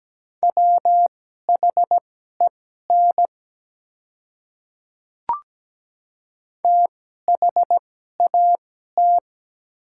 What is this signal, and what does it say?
Morse code, heard as a keyed tone, about 17 wpm, spaced as for 12 wpm, 700 Hz. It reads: WHEN THAT
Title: 2x Character Spacing - Top 300 Words - 17wpm